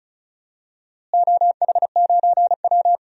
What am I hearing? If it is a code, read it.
Morse code, heard as a keyed tone, about 34 wpm, 700 Hz. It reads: OH9W
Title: DXCC-100 Call Signs - 4 Characters - Repeated 3x - 35wpm